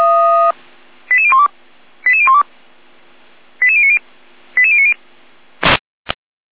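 Five-tone selective call (ZVEI), heard as quick runs of five tones, tones of about 70 ms. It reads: 88012 88012 88090 88090
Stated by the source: ZVEI five-tone call received by SDR.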